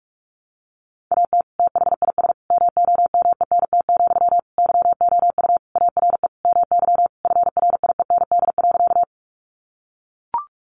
Morse code, heard as a keyed tone, about 44 wpm, 700 Hz. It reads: AT THIS MOMENT, YOU ARE MY FRIEND.
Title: Sentences from Top 1000 Words 45wpm